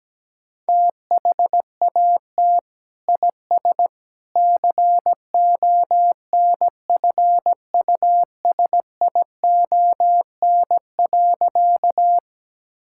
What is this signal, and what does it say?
Morse code, heard as a keyed tone, about 17 wpm, 700 Hz. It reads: THAT IS CONFUSION.
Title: Sentences from Top 1600 Words 17wpm